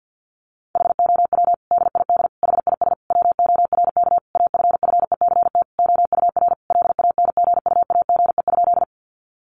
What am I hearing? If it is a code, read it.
Morse code, heard as a keyed tone, about 50 wpm, 700 Hz. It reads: HOW DID HIS WORK AFFECT OUR LANGUAGE?
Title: Sentences from Top 900 Words 50wpm